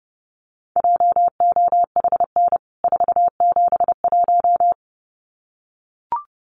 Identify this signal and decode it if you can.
Morse code, heard as a keyed tone, about 30 wpm, 700 Hz. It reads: JOHN 471
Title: CWT Exchange 30wpm